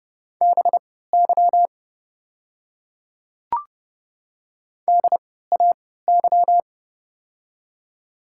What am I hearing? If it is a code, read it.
Morse code, heard as a keyed tone, about 32 wpm, spaced as for 17 wpm, 700 Hz. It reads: BY DAY
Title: 3x Character Spacing - Top 100 Words - 30wpm